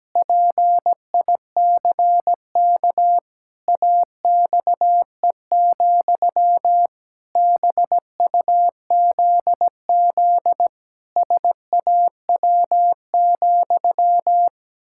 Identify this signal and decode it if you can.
Morse code, heard as a keyed tone, about 17 wpm, 700 Hz. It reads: PICK AXE, BUZZ SAW,
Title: Pangram Challenge v2 - 17wpm